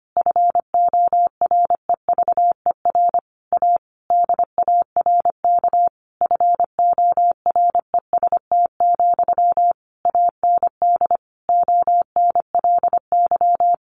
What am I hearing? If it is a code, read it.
Morse code, heard as a keyed tone, about 25 wpm, 700 Hz. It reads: FOREVER A DARK FOREST, AND ONLY
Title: A Sound Of Thunder 25wpm